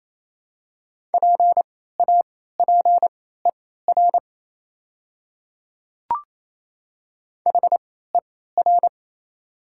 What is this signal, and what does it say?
Morse code, heard as a keyed tone, about 28 wpm, spaced as for 16 wpm, 700 Hz. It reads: PAPER HER
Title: 3x Character Spacing - Top 500 Words - 28wpm